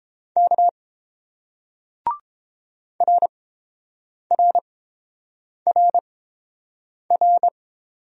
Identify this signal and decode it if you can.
Morse code, heard as a keyed tone, about 29 wpm, spaced as for 7 wpm, 700 Hz. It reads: K RRRR